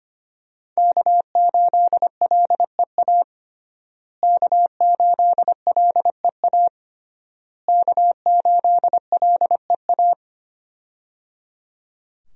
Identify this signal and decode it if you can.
Morse code, heard as a keyed tone, about 25 wpm, 700 Hz. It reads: K8LEA K8LEA K8LEA
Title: USA Call Signs - 5 Characters - Repeated 3x - 25wpm